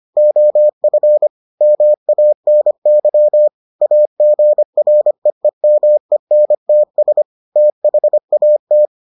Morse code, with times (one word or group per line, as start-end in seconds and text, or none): 0.2-1.3 OF
1.6-3.5 MANY
3.8-7.2 AGREEMENTS
7.6-8.9 THAT